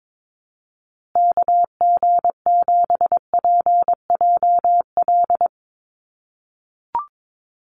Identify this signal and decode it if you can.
Morse code, heard as a keyed tone, about 22 wpm, 700 Hz. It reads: KG7PJL
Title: USA Call Signs Repeated 3x 22wpm